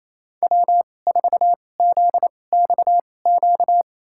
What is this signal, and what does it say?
Morse code, heard as a keyed tone, about 28 wpm, spaced as for 20 wpm, 700 Hz. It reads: W4ZXQ
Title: USA Call Signs - 5 Characters - 2x Character Spacing - 28wpm